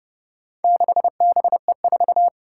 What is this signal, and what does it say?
Morse code, heard as a keyed tone, about 30 wpm, 700 Hz. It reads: 6BE4
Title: Four Challenging Letter-Number Combinations 30wpm